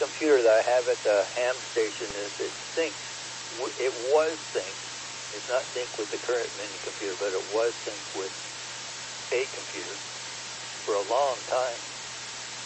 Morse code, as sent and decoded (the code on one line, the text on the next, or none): none